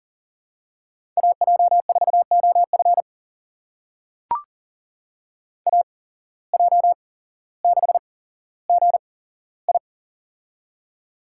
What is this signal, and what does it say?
Morse code, heard as a keyed tone, about 40 wpm, spaced as for 17 wpm, 700 Hz. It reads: AJ4OF AJ6GI
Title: USA Call Signs - 5 Characters - 8x Character Spacing - 40wpm